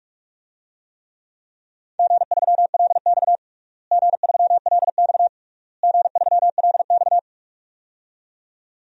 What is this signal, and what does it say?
Morse code, heard as a keyed tone, about 46 wpm, 700 Hz. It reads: G3LX G3LX G3LX
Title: DXCC-100 Call Signs - 4 Characters - Repeated 3x - 45wpm